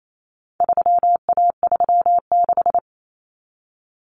Morse code, heard as a keyed tone, about 28 wpm, 700 Hz. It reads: 3A36
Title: Common Cut Numbers - 4-Digit 28wpm